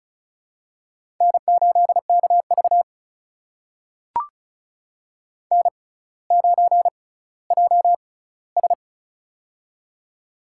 Morse code, heard as a keyed tone, about 36 wpm, spaced as for 18 wpm, 700 Hz. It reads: N8KV N9JS
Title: USA Call Signs - 4 Characters - 6x Character Spacing - 35wpm